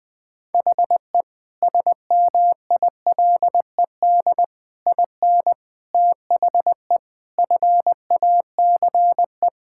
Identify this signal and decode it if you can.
Morse code, heard as a keyed tone, about 20 wpm, 690 Hz. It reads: HE SMILED IN THE FACE
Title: Sentences with 4-Word Phrases - Spoken Beginning - 20wpm